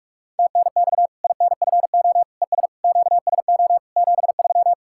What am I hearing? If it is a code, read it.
Morse code, heard as a keyed tone, about 46 wpm, 700 Hz. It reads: TNX INFO ES QSO 73
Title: K1USN SST QSO Practice 45wpm